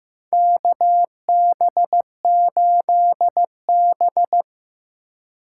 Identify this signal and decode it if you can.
Morse code, heard as a keyed tone, about 15 wpm, 700 Hz. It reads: KB8B
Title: USA Call Signs - 4 Characters - 3x Character Spacing - 15wpm